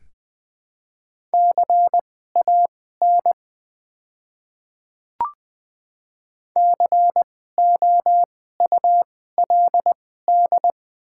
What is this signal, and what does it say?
Morse code, heard as a keyed tone, about 21 wpm, spaced as for 15 wpm, 700 Hz. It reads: CAN COULD
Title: 2x Character Spacing - Top 100 Words - 20wpm